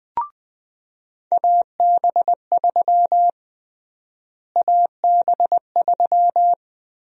Morse code, heard as a keyed tone, about 20 wpm, 700 Hz. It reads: AB3 AB3
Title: USA Call Sign Prefixes - Repeated 2x - 20wpm